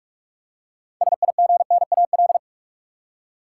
Morse code, heard as a keyed tone, about 46 wpm, 700 Hz. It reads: SIGNAL
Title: Six Letter Words - 45wpm